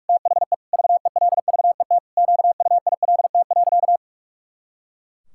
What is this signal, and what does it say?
Morse code, heard as a keyed tone, about 46 wpm, 700 Hz. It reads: THE VELVET QUILT.
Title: Pangram Challenge v2 - 45wpm